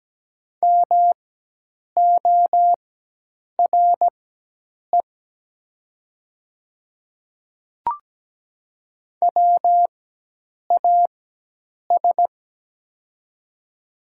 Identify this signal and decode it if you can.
Morse code, heard as a keyed tone, about 17 wpm, spaced as for 8 wpm, 700 Hz. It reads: MORE WAS
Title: 4x Character Spacing - Top 300 Words - 17wpm